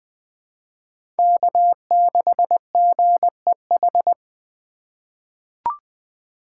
Morse code, heard as a keyed tone, about 20 wpm, 700 Hz. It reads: K6GEH
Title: USA Call Signs - 3x Character Spacing - 20wpm